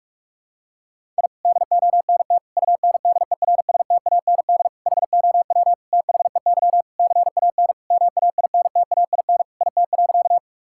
Morse code, heard as a keyed tone, about 46 wpm, 700 Hz. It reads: I DONT UNDERSTAND HOW THEY CAN MAINTAIN IT.
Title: Sentences from Top 1000 Words - Review Part 5 - 45wpm